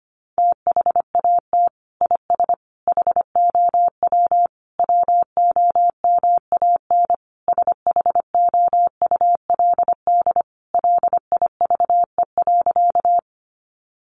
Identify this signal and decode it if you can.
Morse code, heard as a keyed tone, about 25 wpm, 700 Hz. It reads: THAT IS HOW WOMAN SHOULD LIVE.
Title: Sentences from Top 1100 Words 25wpm